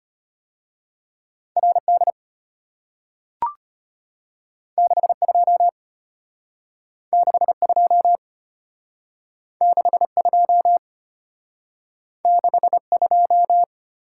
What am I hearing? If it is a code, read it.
Morse code, heard as a keyed tone, about 34 wpm, 700 Hz. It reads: RD 62 62 62 62